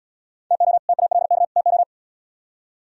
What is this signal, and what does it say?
Morse code, heard as a keyed tone, about 78 wpm, 700 Hz. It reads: TO TALK TO